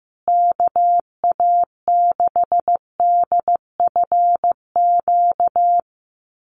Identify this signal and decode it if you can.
Morse code, heard as a keyed tone, about 15 wpm, 700 Hz. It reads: KA6DFQ
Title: USA Call Signs Repeated 2x 15wpm